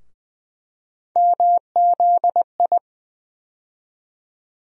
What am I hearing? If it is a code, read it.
Morse code, heard as a keyed tone, about 20 wpm, 700 Hz. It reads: MZI
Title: USA Call Sign Postfixes - 3 Letters 20wpm